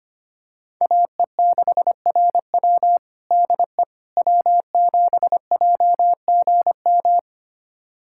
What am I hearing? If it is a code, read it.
Morse code, heard as a keyed tone, about 25 wpm, 700 Hz. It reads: AE6RW DE W7JGM
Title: USA Callsign DE Callsign - 5 Characters - 25wpm